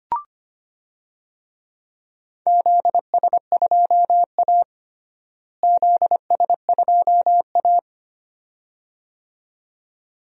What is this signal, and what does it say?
Morse code, heard as a keyed tone, about 25 wpm, 700 Hz. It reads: ZS2A ZS2A
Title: DXCC-100 Call Signs - 4 Characters - Repeated 2x - 25wpm